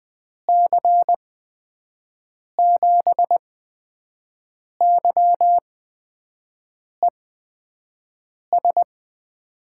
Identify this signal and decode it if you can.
Morse code, heard as a keyed tone, about 20 wpm, spaced as for 5 wpm, 700 Hz. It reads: C7YES
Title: USA Call Signs - 8x Character Spacing - 20wpm